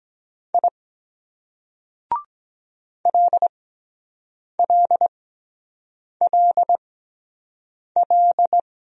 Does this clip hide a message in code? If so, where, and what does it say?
Yes: Morse code from 0.5 to 8.6 s, heard as a keyed tone, about 23 wpm, spaced as for 7 wpm, 700 Hz. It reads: I LLLL